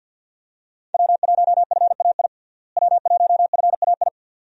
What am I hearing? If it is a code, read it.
Morse code, heard as a keyed tone, about 46 wpm, 700 Hz. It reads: W1FAI W1FAI
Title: USA Call Signs - 5 Characters - Repeated 2x - 50wpm